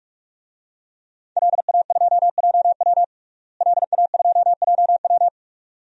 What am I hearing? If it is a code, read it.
Morse code, heard as a keyed tone, about 46 wpm, 700 Hz. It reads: LA2JW LA2JW
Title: DXCC-100 Call Signs - 5 Characters - Repeated 2x - 45wpm